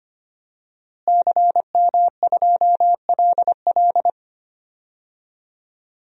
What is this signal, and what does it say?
Morse code, heard as a keyed tone, about 25 wpm, 700 Hz. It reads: CM2LL
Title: DXCC-100 Call Signs - 5 Characters - Repeated 2x - 25wpm